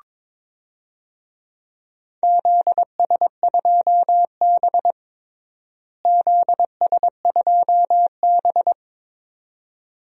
Morse code, heard as a keyed tone, about 22 wpm, 700 Hz. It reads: ZS2B ZS2B
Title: DXCC-100 Call Signs - 4 Characters - Repeated 2x - 22wpm